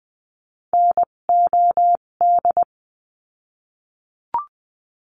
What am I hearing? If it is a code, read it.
Morse code, heard as a keyed tone, about 20 wpm, spaced as for 17 wpm, 700 Hz. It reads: NOD
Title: 257 Lesson 42b - Words 14-20wpm